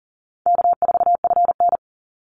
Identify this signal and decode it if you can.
Morse code, heard as a keyed tone, about 40 wpm, 700 Hz. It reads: K4FN
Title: Call Signs - 40wpm